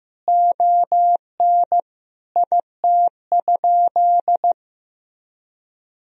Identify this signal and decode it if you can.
Morse code, heard as a keyed tone, about 16 wpm, 700 Hz. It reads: ON IT?
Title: Sentences from Top 1000 Words - Review Part 4 - 15wpm